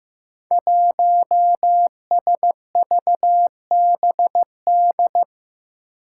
Morse code, heard as a keyed tone, about 15 wpm, 700 Hz. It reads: 1SVBD
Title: Five Challenging Letter-Number Combinations 15wpm